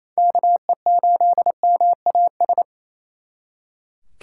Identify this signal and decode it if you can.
Morse code, heard as a keyed tone, about 35 wpm, 700 Hz. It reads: KE8MAS